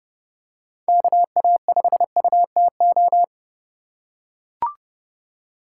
Morse code, heard as a keyed tone, about 30 wpm, 700 Hz. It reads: KA5UTO
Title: USA Call Signs - 3x Character Spacing - 30wpm